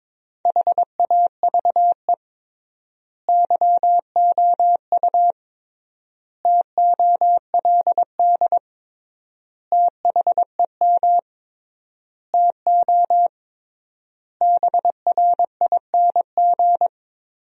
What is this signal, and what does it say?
Morse code, heard as a keyed tone, about 22 wpm, 700 Hz. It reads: HAVE YOU TOLD THEM TO BRING
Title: 3x Word-Spacing - Sentences from Top 500 Words - 22wpm